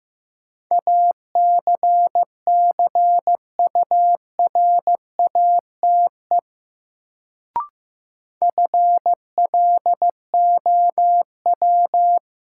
Morse code, heard as a keyed tone, about 15 wpm, 700 Hz. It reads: ACCURATE FLOW
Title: Sentences from Top 1200 Words - New Word Drill - 15wpm